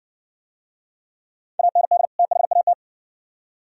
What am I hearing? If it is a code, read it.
Morse code, heard as a keyed tone, about 60 wpm, 700 Hz. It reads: AND THAT